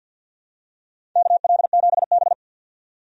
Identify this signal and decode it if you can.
Morse code, heard as a keyed tone, about 52 wpm, 700 Hz. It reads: KL7B
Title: DXCC-100 Call Signs - 4 Characters - 50wpm